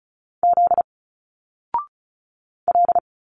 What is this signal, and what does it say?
Morse code, heard as a keyed tone, about 34 wpm, 700 Hz. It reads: Z L
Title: Single Letters 35wpm